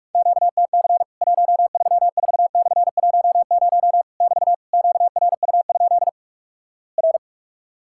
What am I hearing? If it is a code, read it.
Morse code, heard as a keyed tone, about 46 wpm, 700 Hz. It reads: QTC 134/10 <BT> QRU? R
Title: Worked All Europe DX Contest - QTC Exchange - 45wpm